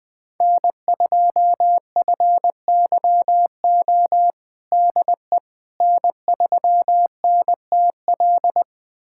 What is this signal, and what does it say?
Morse code, heard as a keyed tone, about 20 wpm, 700 Hz. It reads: N2FYO DE N3NTL
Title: USA Callsign DE Callsign - 5 Characters - 20wpm